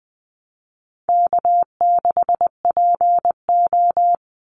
Morse code, heard as a keyed tone, about 20 wpm, 700 Hz. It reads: K6PO
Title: Call Signs Repeated 2x - 20wpm